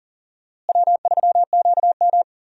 Morse code, heard as a keyed tone, about 40 wpm, 700 Hz. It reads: W3QM
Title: USA Call Signs - 4 Characters - Repeated 3x - 40wpm